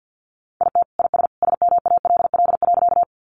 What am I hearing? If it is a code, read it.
Morse code, heard as a keyed tone, about 50 wpm, 700 Hz. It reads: IT IS SMALL.
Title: Additional Easy Phrases 50wpm